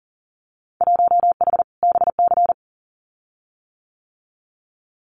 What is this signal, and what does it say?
Morse code, heard as a keyed tone, about 38 wpm, 700 Hz. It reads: 1H BC
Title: Winter Field Day Exchanges 40wpm